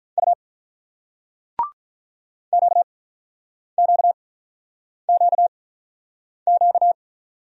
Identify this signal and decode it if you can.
Morse code, heard as a keyed tone, about 46 wpm, spaced as for 9 wpm, 700 Hz. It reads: U QQQQ